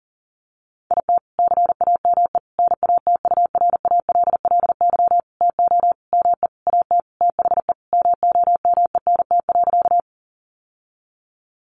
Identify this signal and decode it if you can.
Morse code, heard as a keyed tone, about 40 wpm, 700 Hz. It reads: IT CAME NATURALLY TO ME AT THE MOMENT.
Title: Sentences from Top 1300 Words 40wpm